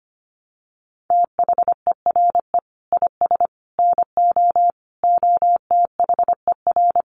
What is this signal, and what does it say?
Morse code, heard as a keyed tone, about 25 wpm, 700 Hz. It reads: THERE IS NO OTHER
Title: Sentences from Top 1300 Words 25wpm